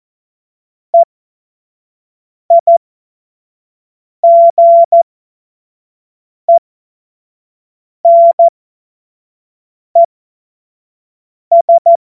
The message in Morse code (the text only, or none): EIGENES